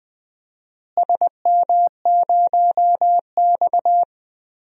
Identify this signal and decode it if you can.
Morse code, heard as a keyed tone, about 20 wpm, 700 Hz. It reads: SM0X
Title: DXCC-100 Call Signs - 4 Characters - Repeated 3x - 20wpm